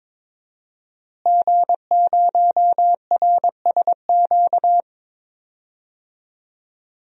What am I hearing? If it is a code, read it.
Morse code, heard as a keyed tone, about 22 wpm, 700 Hz. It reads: G0RSQ
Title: DXCC-100 Call Signs - 5 Characters - Repeated 3x - 22wpm